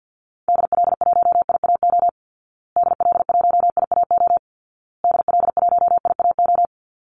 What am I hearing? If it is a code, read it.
Morse code, heard as a keyed tone, about 50 wpm, 700 Hz. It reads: DL1IAO DL1IAO DL1IAO
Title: Call Signs Repeated 2x - 50wpm